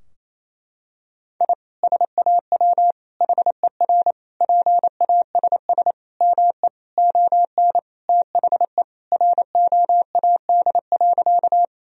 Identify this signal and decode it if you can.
Morse code, heard as a keyed tone, about 28 wpm, 700 Hz. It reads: I SAW HER PASS ME ON THE ROAD.
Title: Sentences from Top 1000 Words - Review Part 2 - 28wpm